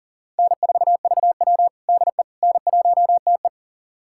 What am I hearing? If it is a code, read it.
Morse code, heard as a keyed tone, about 40 wpm, 700 Hz. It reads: N4VW DE N1TE
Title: USA Callsign DE Callsign - 4 Characters - 40wpm